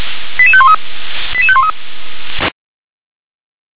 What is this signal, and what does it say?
Five-tone selective call (ZVEI), heard as quick runs of five tones, tones of about 70 ms. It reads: 88513 88513